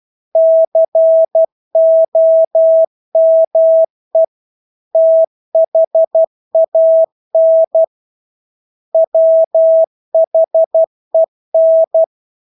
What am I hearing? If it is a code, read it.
Morse code, heard as a keyed tone, about 12 wpm, 650 Hz. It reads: COME THAN WHEN
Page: Engelske ord 12 wpm